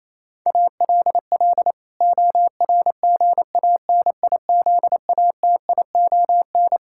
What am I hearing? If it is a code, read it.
Morse code, heard as a keyed tone, about 28 wpm, 700 Hz. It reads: ALL ORGANIZATION
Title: Sentences from Top 1000 Words - Review Part 5 - 28wpm